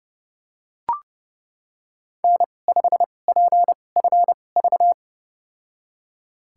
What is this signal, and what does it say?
Morse code, heard as a keyed tone, about 31 wpm, spaced as for 22 wpm, 700 Hz. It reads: N5PFV
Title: USA Call Signs - 2x Character Spacing - 30wpm